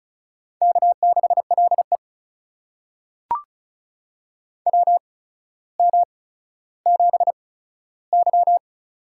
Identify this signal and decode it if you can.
Morse code, heard as a keyed tone, about 36 wpm, spaced as for 15 wpm, 700 Hz. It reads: K6LE WM7Y
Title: USA Call Signs - 4 Characters - 8x Character Spacing - 35wpm